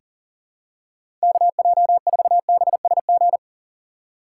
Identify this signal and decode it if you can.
Morse code, heard as a keyed tone, about 40 wpm, 700 Hz. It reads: KJ4BSG
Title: USA Call Signs - 6 Characters - 3x Character Spacing - 40wpm